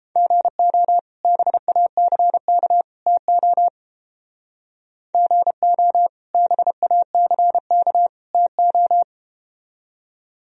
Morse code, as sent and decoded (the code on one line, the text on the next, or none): --. --- / -... .- -.-. -.- / - --- / --. --- / -... .- -.-. -.- / - ---
GO BACK TO GO BACK TO